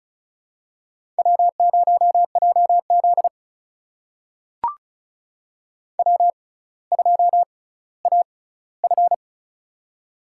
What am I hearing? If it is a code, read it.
Morse code, heard as a keyed tone, about 36 wpm, spaced as for 18 wpm, 700 Hz. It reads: W0JZ W2AF
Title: USA Call Signs - 4 Characters - 6x Character Spacing - 35wpm